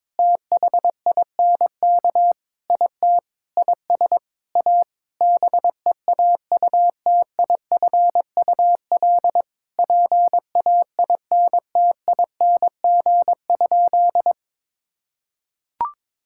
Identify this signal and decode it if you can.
Morse code, heard as a keyed tone, about 21 wpm, 700 Hz. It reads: THINK IT IS A BEAUTIFUL PAINTING?
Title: Sentences from Top 1000 Words - Review Part 6 - 22wpm